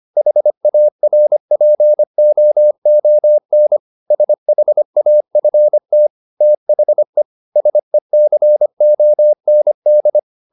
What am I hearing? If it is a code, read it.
Morse code, heard as a keyed tone, about 25 wpm, 600 Hz. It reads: HARPOON SHAFT THE SECOND